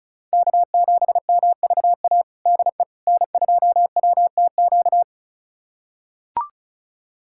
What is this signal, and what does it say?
Morse code, heard as a keyed tone, about 36 wpm, 700 Hz. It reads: K7MVA DE N2WTQ
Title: USA Callsign DE Callsign - 5 Characters - 35wpm